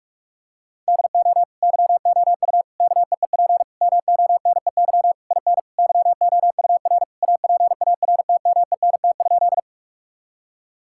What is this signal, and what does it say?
Morse code, heard as a keyed tone, about 46 wpm, 700 Hz. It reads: DO YOU KEEP MONEY IN YOUR APARTMENT?
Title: Sentences from Top 1000 Words - Review Part 6 - 45wpm